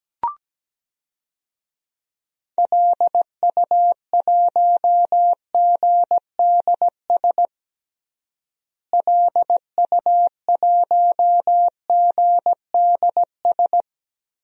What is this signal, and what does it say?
Morse code, heard as a keyed tone, about 17 wpm, 700 Hz. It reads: LU1GDS LU1GDS
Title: DXCC-100 Call Signs - 6 Characters - Repeated 3x - 17wpm